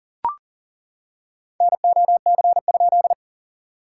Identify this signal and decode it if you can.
Morse code, heard as a keyed tone, about 40 wpm, 700 Hz. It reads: NOC?
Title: SOTA QSO Practice 40wpm